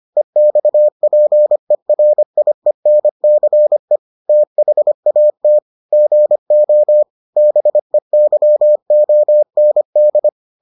Morse code, heard as a keyed tone, about 25 wpm, 600 Hz. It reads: EXPERIENCE THAT GO BEYOND